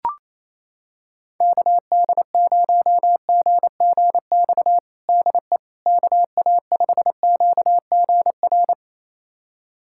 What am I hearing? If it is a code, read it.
Morse code, heard as a keyed tone, about 28 wpm, 700 Hz. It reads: KD0GGX DE KA5QGR
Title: USA Callsign DE Callsign 28wpm